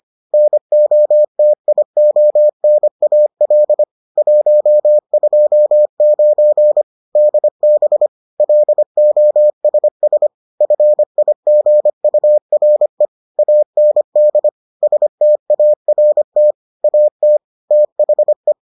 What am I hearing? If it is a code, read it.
Morse code, heard as a keyed tone, about 25 wpm, 600 Hz. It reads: NOTIONAL 129 DB LOSS FIGURE AND START AT THE